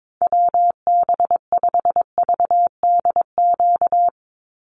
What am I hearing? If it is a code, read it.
Morse code, heard as a keyed tone, about 22 wpm, 700 Hz. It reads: WB5VDQ
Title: USA Call Signs 22wpm